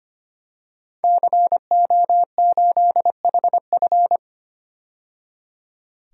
Morse code, heard as a keyed tone, about 25 wpm, 700 Hz. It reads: CO8HF
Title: DXCC-100 Call Signs - 5 Characters - 25wpm